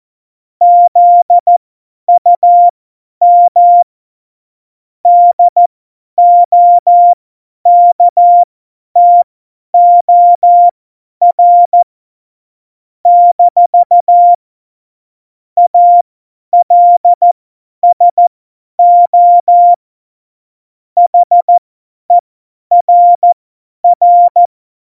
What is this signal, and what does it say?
Morse code, heard as a keyed tone, about 14 wpm, spaced as for 10 wpm, 700 Hz. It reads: ZUM DOKTOR - ALSO HERR
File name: Morsekurs / Witze / Witz 22. September 2020, 50 F70